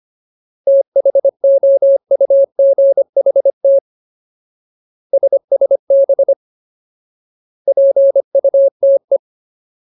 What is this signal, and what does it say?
Morse code, heard as a keyed tone, about 25 wpm, 550 Hz. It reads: THOUGHT SSB PUTE
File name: Ultimate Word List / 25WPM Ultimate